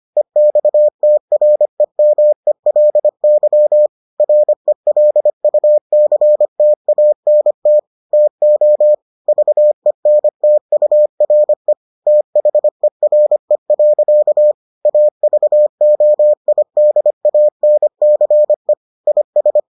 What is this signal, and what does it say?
Morse code, heard as a keyed tone, about 25 wpm, 600 Hz. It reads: EXTREMELY RELUCTANT TO VENTURE THERE. AVOIDANCE IS